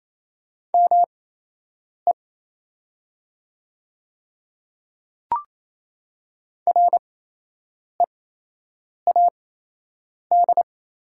Morse code, heard as a keyed tone, about 28 wpm, spaced as for 8 wpm, 690 Hz. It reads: ME READ
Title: 8x Character Spacing - Top 300 Words - 28wpm